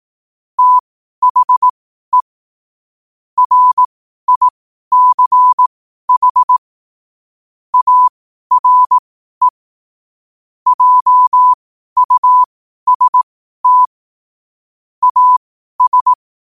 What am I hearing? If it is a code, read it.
Morse code, heard as a keyed tone, about 18 wpm, spaced as for 12 wpm, 1000 Hz. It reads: THE RICH ARE JUST AS
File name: qotdfiles / 12wpm